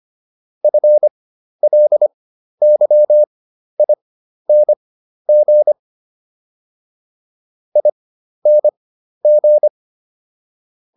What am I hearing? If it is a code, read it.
Morse code, heard as a keyed tone, about 25 wpm, spaced as for 12 wpm, 600 Hz. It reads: FLYING ING